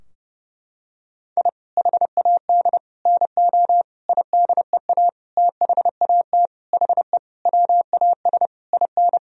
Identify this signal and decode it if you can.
Morse code, heard as a keyed tone, about 32 wpm, 700 Hz. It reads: I HAD NO IDEA THAT HE WAS IN
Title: Sentences from Top 1000 Words - Review Part 2 - 30wpm